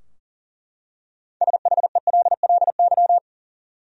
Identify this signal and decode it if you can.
Morse code, heard as a keyed tone, about 40 wpm, 700 Hz. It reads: SHELLY
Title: USA Common Names - 40wpm